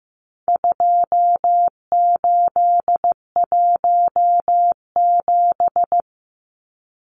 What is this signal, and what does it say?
Morse code, heard as a keyed tone, about 15 wpm, 700 Hz. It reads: 2817
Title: Numbers - 4-Digit 15wpm